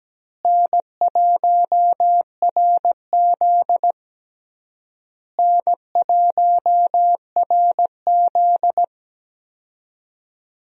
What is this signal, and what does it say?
Morse code, heard as a keyed tone, about 17 wpm, 700 Hz. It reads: N1RZ N1RZ
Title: USA Call Signs - 4 Characters - Repeated 3x - 17wpm